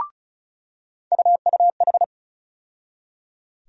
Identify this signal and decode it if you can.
Morse code, heard as a keyed tone, about 36 wpm, 700 Hz. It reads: UUH
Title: USA Call Sign Postfixes - 3 Letters 35wpm